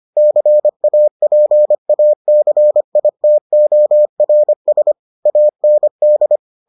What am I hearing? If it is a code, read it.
Morse code, heard as a keyed tone, about 25 wpm, 600 Hz. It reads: CAPACITORS AND